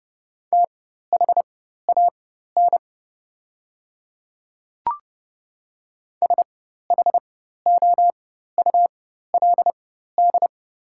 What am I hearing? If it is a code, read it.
Morse code, heard as a keyed tone, about 30 wpm, spaced as for 14 wpm, 690 Hz. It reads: THAN SHOULD
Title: 4x Character Spacing - Top 500 Words - 30wpm